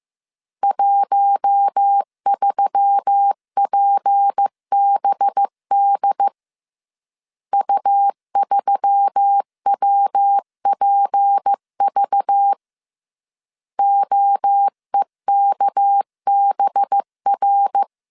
Morse code, heard as a keyed tone, about 15 wpm, 800 Hz. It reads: 13PBD U3WPV OEKBR
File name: CW / DET / 1 6 1500C S C 14wpm 2951